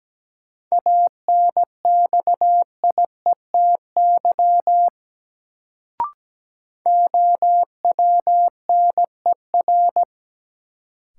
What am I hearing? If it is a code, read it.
Morse code, heard as a keyed tone, about 17 wpm, 700 Hz. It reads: ANXIETY OWNER
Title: Sentences from Top 1600 Words - New Word Drill - 17wpm